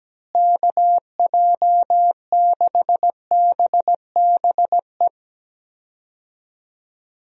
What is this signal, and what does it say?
Morse code, heard as a keyed tone, about 17 wpm, 700 Hz. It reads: KJ6BBE
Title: USA Call Signs - 6 Characters - Repeated 3x - 17wpm